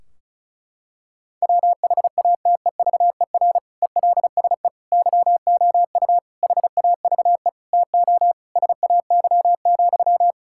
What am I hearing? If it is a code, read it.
Morse code, heard as a keyed tone, about 36 wpm, 700 Hz. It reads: WHATEVER ELSE YOU HAVE TO SAY,
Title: Sentences from Top 1600 Words 35wpm